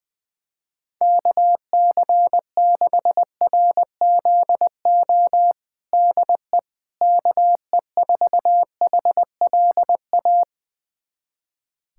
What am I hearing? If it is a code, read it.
Morse code, heard as a keyed tone, about 21 wpm, 700 Hz. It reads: KC6RZO DE KE4HLA
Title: USA Callsign DE Callsign - 6 Characters - 20wpm